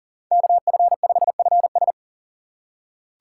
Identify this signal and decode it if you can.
Morse code, heard as a keyed tone, about 37 wpm, 700 Hz. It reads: KF5FS
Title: USA Call Signs - 5 Characters - 3x Character Spacing - 40wpm